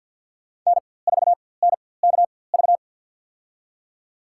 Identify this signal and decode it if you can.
Morse code, heard as a keyed tone, about 53 wpm, spaced as for 24 wpm, 700 Hz. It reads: N4NKV
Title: USA Call Signs - 5 Characters - 4x Character Spacing - 50wpm